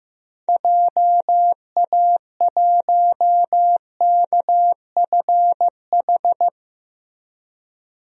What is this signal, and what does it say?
Morse code, heard as a keyed tone, about 15 wpm, 700 Hz. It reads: JA1KFH
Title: DXCC-100 Call Signs - 6 Characters - Repeated 3x - 15wpm